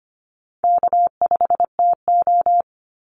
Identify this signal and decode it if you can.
Morse code, heard as a keyed tone, about 25 wpm, 700 Hz. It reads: K5TO
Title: Call Signs - 25wpm